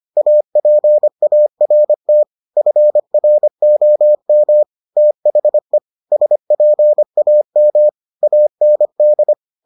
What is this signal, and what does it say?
Morse code, heard as a keyed tone, about 25 wpm, 600 Hz. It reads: APART FROM THE SPAM AND